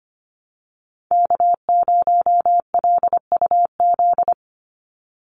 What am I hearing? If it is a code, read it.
Morse code, heard as a keyed tone, about 25 wpm, 700 Hz. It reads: K0LUZ
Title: Call Signs Repeated 2x - 25wpm